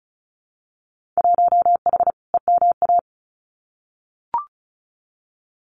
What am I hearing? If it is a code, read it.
Morse code, heard as a keyed tone, about 36 wpm, 700 Hz. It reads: 1H EMA
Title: Winter Field Day Exchanges 35wpm